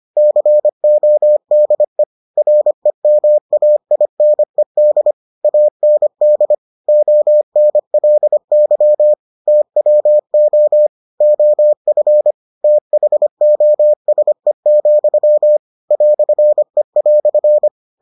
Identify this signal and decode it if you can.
Morse code, heard as a keyed tone, about 25 wpm, 600 Hz. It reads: CODE REMAINED AND ONLY TWO OF THOSE, "E"